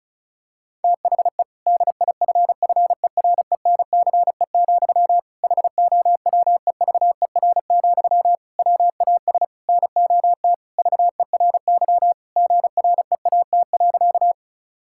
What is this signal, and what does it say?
Morse code, heard as a keyed tone, about 36 wpm, 700 Hz. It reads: THE DIFFERENCE, HOWEVER, WAS NOT VERY GREAT.
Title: Sentences from Top 1000 Words - Review Part 5 - 35wpm